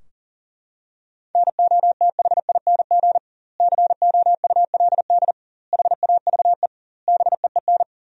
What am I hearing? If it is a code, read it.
Morse code, heard as a keyed tone, about 40 wpm, 700 Hz. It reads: NOTHING COULD HAVE BEEN
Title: 2x Word-Spacing - Sentences from Top 500 Words - 40wpm